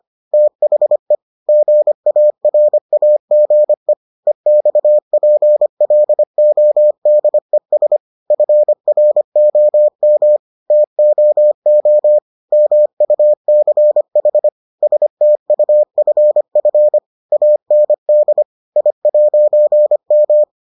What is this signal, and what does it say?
Morse code, heard as a keyed tone, about 25 wpm, 600 Hz. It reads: THE GARAGE EXPLODES FROM TOO MUCH STUFF AND I'M